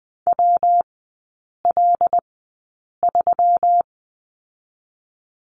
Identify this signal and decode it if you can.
Morse code, heard as a keyed tone, about 20 wpm, spaced as for 8 wpm, 700 Hz. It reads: WL3
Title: 121 Lesson 20e - Number 3 - Three Characters 20wpm